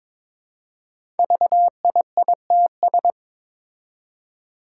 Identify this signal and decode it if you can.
Morse code, heard as a keyed tone, about 22 wpm, 700 Hz. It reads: VIITS
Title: Five Challenging Letter-Number Combinations 22wpm